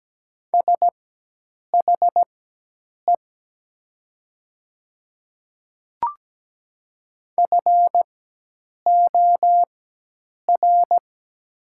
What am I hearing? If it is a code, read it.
Morse code, heard as a keyed tone, about 17 wpm, spaced as for 8 wpm, 700 Hz. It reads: SHE FOR
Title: 4x Character Spacing - Top 200 Words - 17wpm